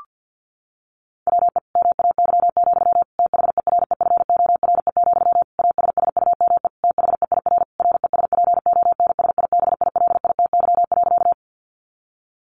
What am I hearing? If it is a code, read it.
Morse code, heard as a keyed tone, about 49 wpm, 700 Hz. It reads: WE MAY, THEREFORE, ASSUME THEIR RESPONSIBILITY.
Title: Sentences from Top 1500 Words 50wpm